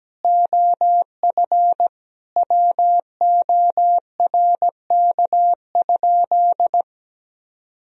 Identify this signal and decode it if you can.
Morse code, heard as a keyed tone, about 17 wpm, 700 Hz. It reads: OF WORK?
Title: Sentences from Top 1000 Words - Review Part 3 - 17wpm